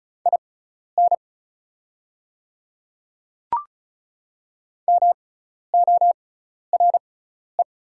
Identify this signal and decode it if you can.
Morse code, heard as a keyed tone, about 34 wpm, spaced as for 12 wpm, 690 Hz. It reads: IN MORE
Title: 6x Character Spacing - Top 200 Words - 35wpm